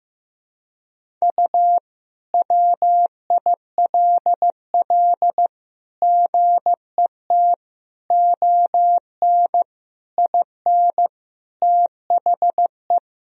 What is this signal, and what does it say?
Morse code, heard as a keyed tone, about 15 wpm, 700 Hz. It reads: U WILL GET ON IN THE
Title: Sentences from Top 200 Words and Abbreviations 15wpm